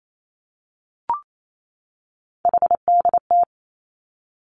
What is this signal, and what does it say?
Morse code, heard as a keyed tone, about 28 wpm, 700 Hz. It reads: HDT
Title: Three Challenging Letter-Number Combinations 28wpm